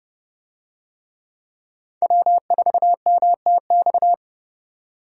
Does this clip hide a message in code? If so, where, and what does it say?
Yes: Morse code from 2.0 to 4.1 s, heard as a keyed tone, about 30 wpm, 700 Hz. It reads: W4MTX